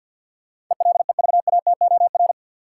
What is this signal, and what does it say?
Morse code, heard as a keyed tone, about 52 wpm, 700 Hz. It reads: ELEVATOR
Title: Sentences from Top 1600 Words - New Word Drill Alt - 50wpm